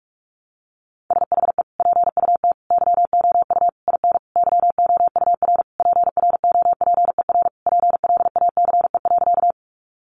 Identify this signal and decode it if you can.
Morse code, heard as a keyed tone, about 44 wpm, 700 Hz. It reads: SHE PUT YOU IN YOUR PROPER PLACE.
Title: Sentences from Top 800 Words 45wpm